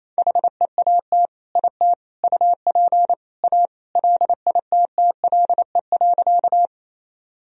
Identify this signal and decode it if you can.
Morse code, heard as a keyed tone, about 28 wpm, 700 Hz. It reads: HEAT IT UP A LITTLE.
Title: Sentences from Top 1000 Words - Review Part 3 - 28wpm